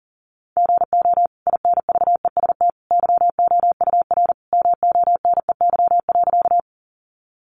Morse code, heard as a keyed tone, about 40 wpm, 700 Hz. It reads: GO INVEST YOUR MONEY.
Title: Sentences from Top 1100 Words 40wpm